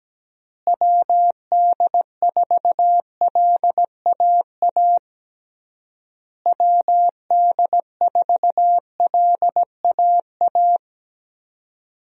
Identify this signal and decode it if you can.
Morse code, heard as a keyed tone, about 17 wpm, 700 Hz. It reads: WD4LAA WD4LAA
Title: USA Call Signs - 6 Characters - Repeated 3x - 17wpm